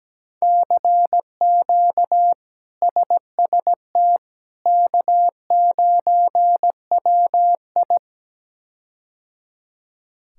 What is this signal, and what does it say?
Morse code, heard as a keyed tone, about 17 wpm, 700 Hz. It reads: CQ SST K9WI
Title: K1USN SST QSO Practice - 17wpm